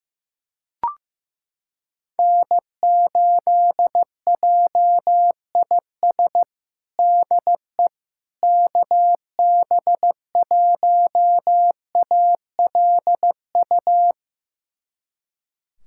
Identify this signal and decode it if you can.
Morse code, heard as a keyed tone, about 15 wpm, 700 Hz. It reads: N8JIS DE KB1ALU
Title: USA Callsign DE Callsign 15wpm